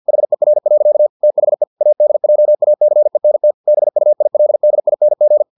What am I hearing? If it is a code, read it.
Morse code, heard as a keyed tone, about 50 wpm, 600 Hz. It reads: HER. THE ADJACENT BUILDING
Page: Never 49wpm